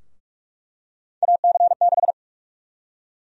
Morse code, heard as a keyed tone, about 46 wpm, 700 Hz. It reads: AC6
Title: USA Call Sign Prefixes - 45wpm